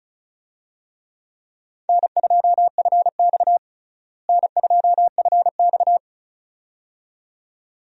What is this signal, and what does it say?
Morse code, heard as a keyed tone, about 34 wpm, 700 Hz. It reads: N2FX N2FX
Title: USA Call Signs - 4 Characters - Repeated 2x - 35wpm